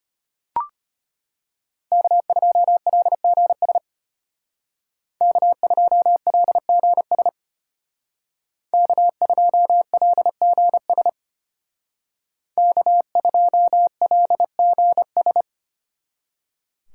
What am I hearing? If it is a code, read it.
Morse code, heard as a keyed tone, about 32 wpm, 700 Hz. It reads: K2LGS K2LGS K2LGS K2LGS